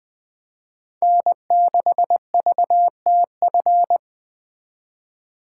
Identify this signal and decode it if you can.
Morse code, heard as a keyed tone, about 20 wpm, 700 Hz. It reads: N6VTF